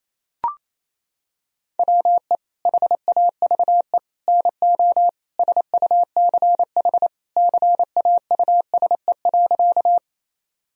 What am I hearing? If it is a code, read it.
Morse code, heard as a keyed tone, about 28 wpm, 700 Hz. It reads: WE HAVE NO SUCH CAUSE.
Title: Sentences from Top 1000 Words - Review Part 1 - 28wpm